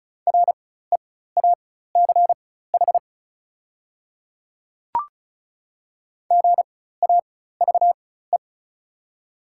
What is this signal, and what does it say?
Morse code, heard as a keyed tone, about 34 wpm, spaced as for 16 wpm, 690 Hz. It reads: REACH GAVE
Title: 4x Character Spacing - Top 500 Words - 35wpm